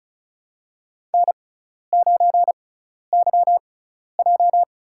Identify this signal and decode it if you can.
Morse code, heard as a keyed tone, about 36 wpm, spaced as for 12 wpm, 700 Hz. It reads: N9YJ
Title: USA Call Signs - 6x Character Spacing - 35wpm